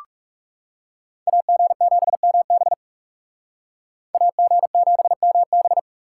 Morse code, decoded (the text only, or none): AG7MB AG7MB